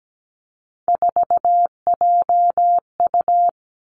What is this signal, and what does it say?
Morse code, heard as a keyed tone, about 17 wpm, 700 Hz. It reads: 4JU
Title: Three Challenging Letter-Number Combinations 17wpm